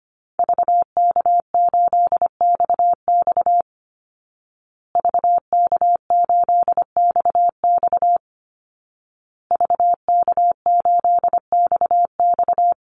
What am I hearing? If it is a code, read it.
Morse code, heard as a keyed tone, about 25 wpm, 700 Hz. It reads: VK8XX VK8XX VK8XX
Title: Call Signs Repeated 2x - 25wpm